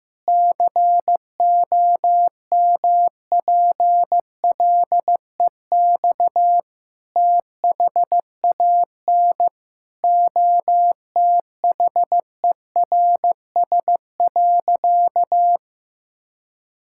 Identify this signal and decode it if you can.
Morse code, heard as a keyed tone, about 16 wpm, 700 Hz. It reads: COMPLEX THAN OTHERS.
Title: Sentences from Top 1000 Words - Review Part 6 - 15wpm